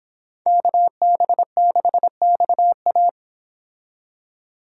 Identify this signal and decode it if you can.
Morse code, heard as a keyed tone, about 30 wpm, 700 Hz. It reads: KB6XA